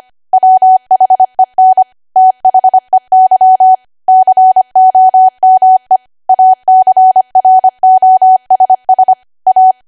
Morse code, heard as a keyed tone, about 25 wpm, 750 Hz. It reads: WHEN THEY COME ACROSS A
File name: Archive / 25 WPM / 25WPM